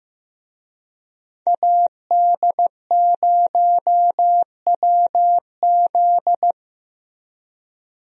Morse code, heard as a keyed tone, about 15 wpm, 700 Hz. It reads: AD0WZ
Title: USA Call Signs - 5 Characters - Repeated 2x - 15wpm